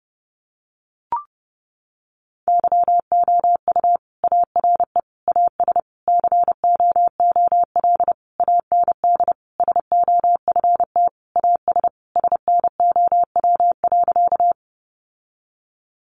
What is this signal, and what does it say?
Morse code, heard as a keyed tone, about 30 wpm, 700 Hz. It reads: YOU ARE AS COOL AND SOFT AS SNOW.
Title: Sentences from Top 1500 Words 30wpm